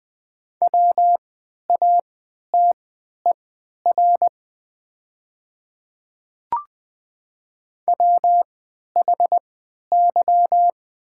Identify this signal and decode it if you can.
Morse code, heard as a keyed tone, about 20 wpm, spaced as for 11 wpm, 700 Hz. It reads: WATER WHY
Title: 3x Character Spacing - Top 200 Words - 20wpm